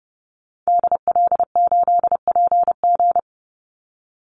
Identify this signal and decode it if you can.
Morse code, heard as a keyed tone, about 30 wpm, 700 Hz. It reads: DL8PG
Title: Call Signs Repeated 2x - 30wpm